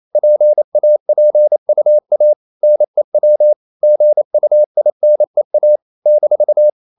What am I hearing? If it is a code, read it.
Morse code, heard as a keyed tone, about 28 wpm, 600 Hz. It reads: PAPUA NEW GUINEA -
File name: CW / Practice / NationalCapitals28